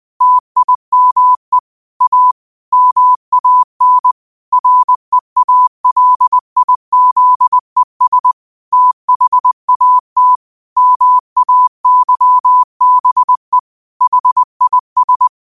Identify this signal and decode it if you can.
Morse code, heard as a keyed tone, about 20 wpm, 1000 Hz. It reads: TIME A MAN REALIZES THAT MAYBE HIS
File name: qotdfiles / 20wpm